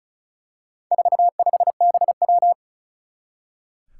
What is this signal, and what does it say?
Morse code, heard as a keyed tone, about 36 wpm, 700 Hz. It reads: 45BW
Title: Four Challenging Letter-Number Combinations 35wpm